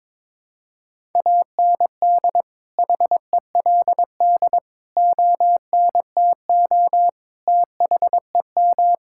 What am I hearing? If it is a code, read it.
Morse code, heard as a keyed tone, about 21 wpm, 700 Hz. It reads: AND HELD ONTO THEM
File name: Sets of 9 Words - 22wpm